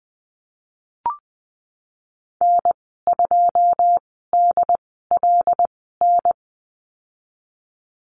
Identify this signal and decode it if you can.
Morse code, heard as a keyed tone, about 20 wpm, spaced as for 14 wpm, 700 Hz. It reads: N2DLN
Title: 154 Lesson 25g - Letter p - Call Signs 20wpm